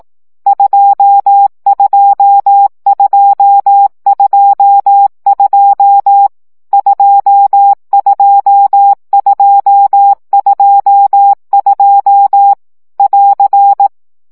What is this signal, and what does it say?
Morse code, heard as a keyed tone, about 18 wpm, 800 Hz. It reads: 22222 22222 <AR>
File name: CW gyakorlas 2 15 baud